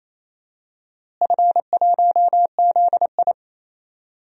Morse code, heard as a keyed tone, about 28 wpm, 700 Hz. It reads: F1ZI
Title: DXCC-100 Call Signs - 4 Characters - Repeated 3x - 28wpm